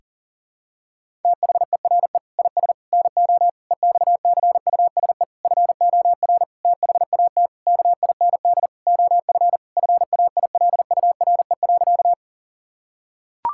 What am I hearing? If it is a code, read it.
Morse code, heard as a keyed tone, about 40 wpm, 700 Hz. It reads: THERE IS NO EXCUSE FOR THAT KIND OF FAILURE.
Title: Sentences from Top 1000 Words - Review Part 6 - 40wpm